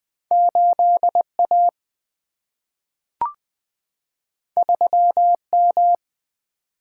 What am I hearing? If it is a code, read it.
Morse code, heard as a keyed tone, about 20 wpm, 700 Hz. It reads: 8A 3M
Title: Single-Letter Single-Number 20wpm